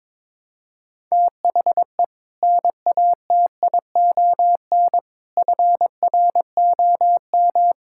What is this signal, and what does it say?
Morse code, heard as a keyed tone, about 21 wpm, 700 Hz. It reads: THE NATION FROM
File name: practice / 22wpm / Build A Phrase - 5-Word Phrases - 22wpm